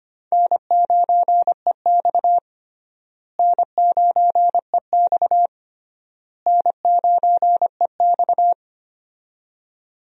Morse code, heard as a keyed tone, about 25 wpm, 700 Hz. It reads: N9EX N9EX N9EX
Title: USA Call Signs - 4 Characters - Repeated 3x - 25wpm